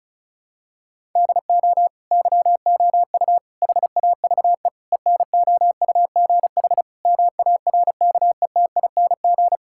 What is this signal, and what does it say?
Morse code, heard as a keyed tone, about 34 wpm, 700 Hz. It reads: DO YOU HAVE ENOUGH MARKETING
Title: Sentences from Top 1000 Words - Review Part 5 - 35wpm